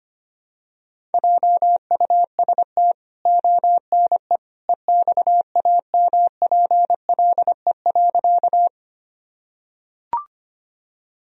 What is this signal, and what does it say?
Morse code, heard as a keyed tone, about 25 wpm, 700 Hz. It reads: JUST ONE EXAMPLE.
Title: Sentences from Top 1000 Words - Review Part 1 - 25wpm